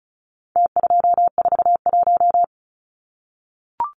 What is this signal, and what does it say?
Morse code, heard as a keyed tone, about 34 wpm, 700 Hz. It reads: T241
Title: Numbers - Contest Style - 4-Digit 35wpm